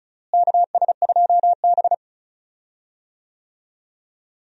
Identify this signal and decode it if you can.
Morse code, heard as a keyed tone, about 36 wpm, 700 Hz. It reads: KS2B
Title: USA Call Signs - 4 Characters - Repeated 3x - 35wpm